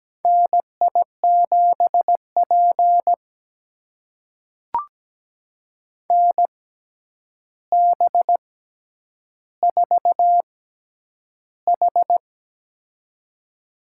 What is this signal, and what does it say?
Morse code, heard as a keyed tone, about 17 wpm, spaced as for 9 wpm, 700 Hz. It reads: NI7P NB4H
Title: USA Call Signs - 4 Characters - 6x Character Spacing - 17wpm